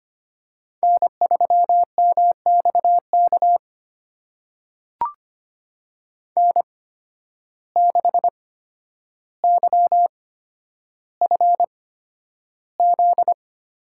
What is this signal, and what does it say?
Morse code, heard as a keyed tone, about 25 wpm, spaced as for 11 wpm, 700 Hz. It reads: N3MXK N6YFZ
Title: USA Call Signs - 5 Characters - 8x Character Spacing - 25wpm